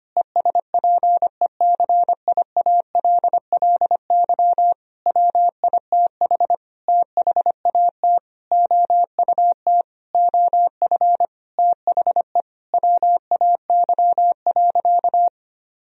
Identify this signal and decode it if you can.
Morse code, heard as a keyed tone, about 25 wpm, 700 Hz. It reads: ESPECIALLY WITH THAT OUT OF THE WAY.
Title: Sentences from Top 1000 Words - Review Part 4 - 25wpm